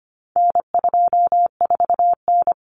Morse code, heard as a keyed tone, about 25 wpm, 700 Hz. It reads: N24N
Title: Common Cut Numbers - 4-Digit 25wpm